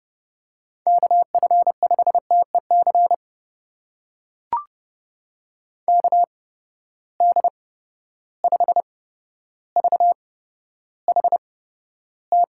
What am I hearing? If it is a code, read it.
Morse code, heard as a keyed tone, about 30 wpm, spaced as for 13 wpm, 700 Hz. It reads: KF5TEC KD5VHT
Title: USA Call Signs - 6 Characters - 8x Character Spacing - 30wpm